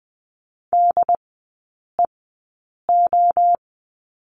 Morse code, heard as a keyed tone, about 19 wpm, spaced as for 8 wpm, 700 Hz. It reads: DEO
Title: 074 Lesson 13e - Letter L - Three Characters 20wpm